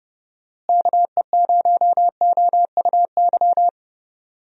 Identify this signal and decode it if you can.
Morse code, heard as a keyed tone, about 32 wpm, 700 Hz. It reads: KE0OUY
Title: USA Call Signs - 6 Characters - Repeated 3x - 30wpm